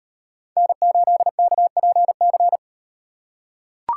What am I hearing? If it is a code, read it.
Morse code, heard as a keyed tone, about 38 wpm, 700 Hz. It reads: N8KPC